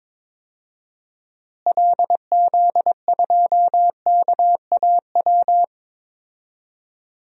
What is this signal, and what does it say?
Morse code, heard as a keyed tone, about 22 wpm, 700 Hz. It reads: LZ2KAW
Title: DXCC-100 Call Signs - 6 Characters - Repeated 2x - 22wpm